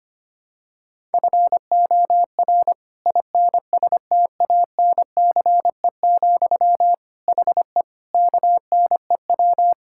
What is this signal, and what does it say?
Morse code, heard as a keyed tone, about 25 wpm, 700 Hz. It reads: FOR INSTANCE, HE KNEW
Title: Sentences from Top 1000 Words - Review Part 5 - 25wpm